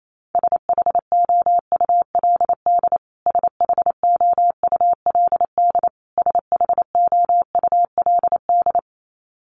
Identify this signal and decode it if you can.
Morse code, heard as a keyed tone, about 28 wpm, 700 Hz. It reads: SHOULD SHOULD SHOULD
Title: Top 500 Words Repeated 28wpm